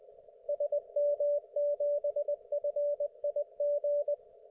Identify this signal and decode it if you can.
Morse code, heard as a keyed tone, about 23 wpm, 560 Hz, with radio noise. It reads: SM7FIG